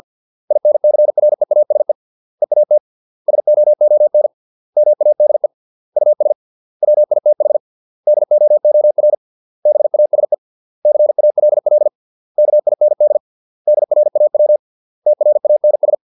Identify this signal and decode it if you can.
Morse code, heard as a keyed tone, about 50 wpm, 600 Hz. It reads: INCREASE EAT SOON MADE US WITH DOOR BASE CALL KIND DRAW TRANS
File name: 632 words / 632 Words 50WPM